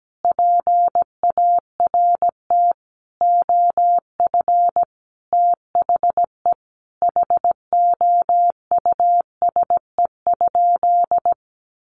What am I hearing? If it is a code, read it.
Morse code, heard as a keyed tone, about 17 wpm, 700 Hz. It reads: PART OF THE HOUSE?
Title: Sentences from Top 200 Words 17wpm